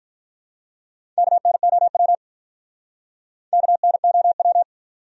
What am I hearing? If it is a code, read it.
Morse code, heard as a keyed tone, about 48 wpm, 700 Hz. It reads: KNOW KNOW